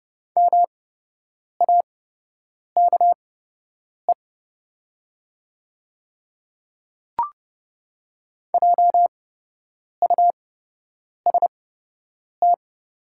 Morse code, heard as a keyed tone, about 29 wpm, spaced as for 8 wpm, 700 Hz. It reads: MAKE JUST
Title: 8x Character Spacing - Top 200 Words - 30wpm